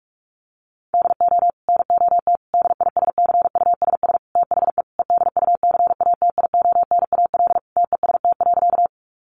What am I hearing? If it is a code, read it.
Morse code, heard as a keyed tone, about 44 wpm, 700 Hz. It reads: DO NOT DISCUSS THE EDUCATIONAL TEST.
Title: Sentences from Top 1200 Words 45wpm